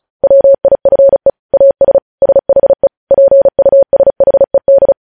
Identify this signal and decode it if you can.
Morse code, heard as a keyed tone, about 36 wpm, 550 Hz. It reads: WIFE AS SHE PUSHED
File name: podcast / News headlines-1x-35x35